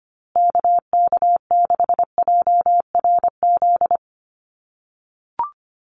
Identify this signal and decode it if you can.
Morse code, heard as a keyed tone, about 25 wpm, 700 Hz. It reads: KK6JRZ
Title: USA Call Signs Repeated 2x 25wpm